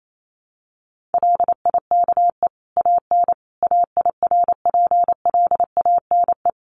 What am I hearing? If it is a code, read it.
Morse code, heard as a keyed tone, about 28 wpm, 700 Hz. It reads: LIKE AN AIRPLANE
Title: A Sound Of Thunder 28wpm